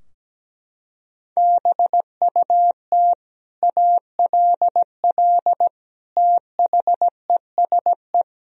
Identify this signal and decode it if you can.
Morse code, heard as a keyed tone, about 17 wpm, 700 Hz. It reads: BUT ALL THESE
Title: Sentences from Top 1000 Words - Review Part 3 - 17wpm